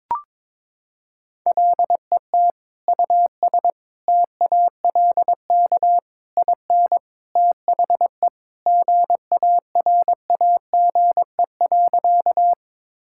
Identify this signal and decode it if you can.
Morse code, heard as a keyed tone, about 22 wpm, 700 Hz. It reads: LET US TALK IN THE GARAGE.
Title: Sentences from Top 1600 Words 22wpm